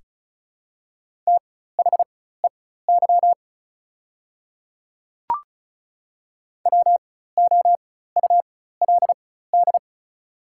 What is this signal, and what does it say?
Morse code, heard as a keyed tone, about 34 wpm, spaced as for 16 wpm, 700 Hz. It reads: THEY WOULD
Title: 4x Character Spacing - Top 100 Words - 35wpm